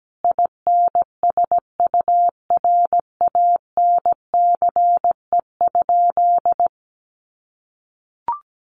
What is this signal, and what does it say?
Morse code, heard as a keyed tone, about 17 wpm, 700 Hz. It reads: INSURANCE?
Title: Sentences from Top 1100 Words 17wpm